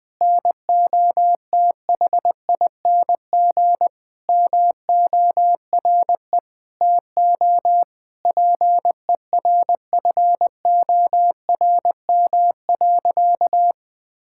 Morse code, heard as a keyed tone, about 20 wpm, 700 Hz. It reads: NOTHING MORE TO PERFORM.
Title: Sentences from Top 1000 Words - Review Part 5 - 20wpm